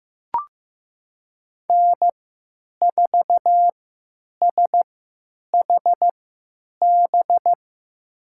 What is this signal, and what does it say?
Morse code, heard as a keyed tone, about 15 wpm, spaced as for 8 wpm, 700 Hz. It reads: N4SHB
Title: USA Call Signs - 5 Characters - 3x Character Spacing - 15wpm